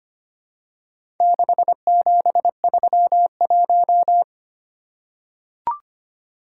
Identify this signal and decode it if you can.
Morse code, heard as a keyed tone, about 25 wpm, 700 Hz. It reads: 6731
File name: practice / 25wpm / Four Letter-Number Combinations - 25wpm